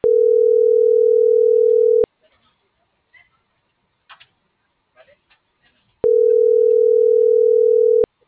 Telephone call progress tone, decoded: ringback tone

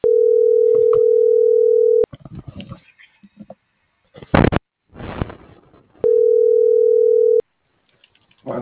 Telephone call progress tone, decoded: ringback tone

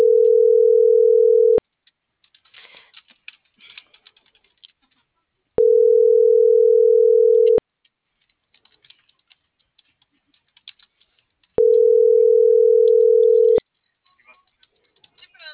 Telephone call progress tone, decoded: ringback tone

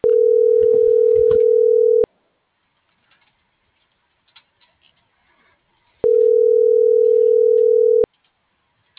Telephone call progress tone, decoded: ringback tone

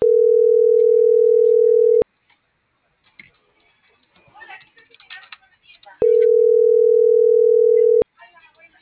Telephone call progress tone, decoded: ringback tone

